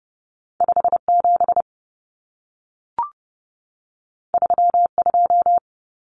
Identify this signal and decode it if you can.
Morse code, heard as a keyed tone, about 30 wpm, 700 Hz. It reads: 57 32
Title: Common Cut Numbers - 2-Digit 30wpm